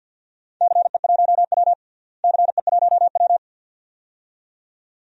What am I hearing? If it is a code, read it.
Morse code, heard as a keyed tone, about 49 wpm, 700 Hz. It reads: CE1W CE1W
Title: DXCC-100 Call Signs - 4 Characters - Repeated 2x - 50wpm